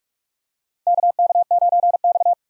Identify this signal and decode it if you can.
Morse code, heard as a keyed tone, about 44 wpm, 700 Hz. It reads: KK9X
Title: USA Call Signs - 4 Characters - 45wpm